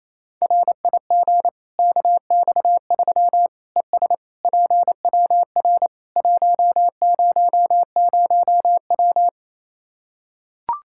Morse code, heard as a keyed tone, about 28 wpm, 700 Hz. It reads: RIG KX3 ES PWR 100W
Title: Practice QSO - Exchange 2 - Info - 28wpm